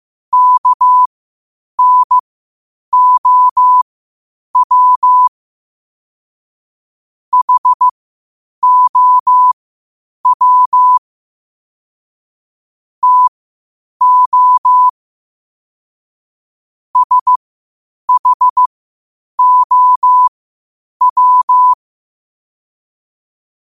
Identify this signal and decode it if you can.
Morse code, heard as a keyed tone, about 15 wpm, spaced as for 8 wpm, 1000 Hz. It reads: KNOW HOW TO SHOW
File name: qotdfiles / 8wpm